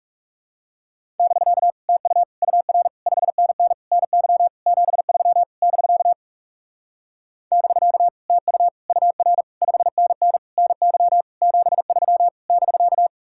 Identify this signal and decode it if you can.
Morse code, heard as a keyed tone, about 38 wpm, 700 Hz. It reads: <BK> TU UR 5NN NY 73 <BK> <BK> TU UR 5NN NY 73 <BK>